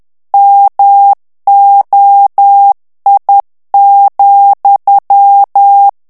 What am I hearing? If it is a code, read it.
Morse code, heard as a keyed tone, about 10 wpm, 800 Hz. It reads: MOI,
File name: FRANCAIS TEXTE CLAIR 2 VITESSE 10 MOTS MINUTES